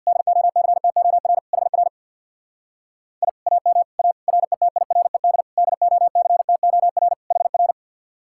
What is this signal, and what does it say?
Morse code, heard as a keyed tone, about 58 wpm, 700 Hz. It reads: DOCTOR HR I AM A RETIRED DOCTOR HR
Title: Speed-Racing Common QSO Phrases - 50wpm